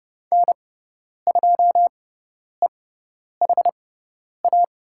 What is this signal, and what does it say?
Morse code, heard as a keyed tone, about 29 wpm, spaced as for 10 wpm, 700 Hz. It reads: N2EHA
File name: Lesson 18g - Letter m - Call Signs 30wpm